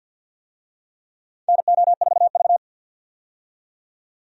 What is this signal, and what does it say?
Morse code, heard as a keyed tone, about 51 wpm, 700 Hz. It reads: NO4V
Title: USA Call Signs - 4 Characters - 50wpm